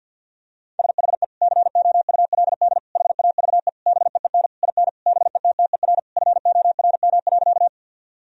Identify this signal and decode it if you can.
Morse code, heard as a keyed tone, about 52 wpm, 700 Hz. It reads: SHE COULD HAVE BEEN IN BETTER FORM.
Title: Sentences from Top 1000 Words - Review Part 1 - 50wpm